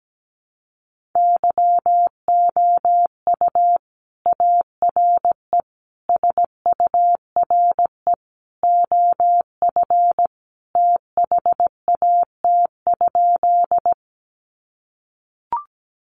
Morse code, heard as a keyed tone, about 17 wpm, 700 Hz. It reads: YOU ARE SURE OF THAT?
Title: Sentences from Top 300 Words 17wpm